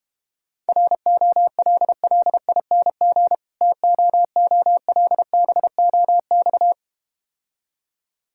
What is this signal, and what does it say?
Morse code, heard as a keyed tone, about 32 wpm, 700 Hz. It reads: ROLLING TOOLBOX